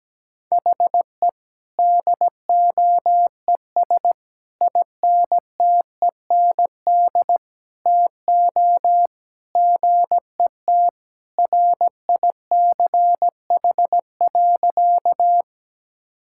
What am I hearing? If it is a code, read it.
Morse code, heard as a keyed tone, about 17 wpm, 700 Hz. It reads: HE DOES INTEND TO GET RICH.
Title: Sentences from Top 1600 Words 17wpm